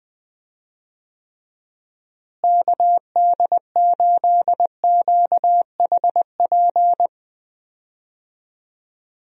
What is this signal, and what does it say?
Morse code, heard as a keyed tone, about 20 wpm, 700 Hz. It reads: KD8QHP
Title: USA Call Signs - 6 Characters - Repeated 2x - 20wpm